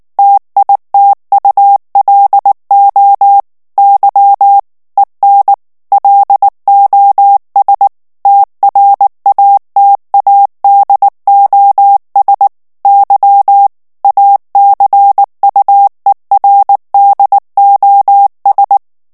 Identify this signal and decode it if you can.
Morse code, heard as a keyed tone, about 19 wpm, 800 Hz. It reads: TITULO Y EN LOS TRATADOS Y ACUERDOS
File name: morse mp3 / ESPAGNOL TEXTE EN CLAIR 2 VITESSE 18 MOTS MINUTES